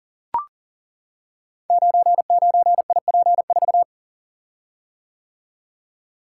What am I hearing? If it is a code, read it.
Morse code, heard as a keyed tone, about 40 wpm, 700 Hz. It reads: 99IP4
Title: Code Groups - 5 Letter:Numbers - 40wpm